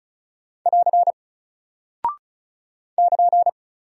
Morse code, heard as a keyed tone, about 36 wpm, 700 Hz. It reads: <AR> <KN>
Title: Single Character (Letter:Number:Punctuation:Prosign) Alt 35wpm